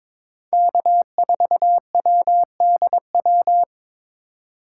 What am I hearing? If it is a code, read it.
Morse code, heard as a keyed tone, about 22 wpm, 700 Hz. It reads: K4WDW
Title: USA Call Signs - 8x Character Spacing - 22wpm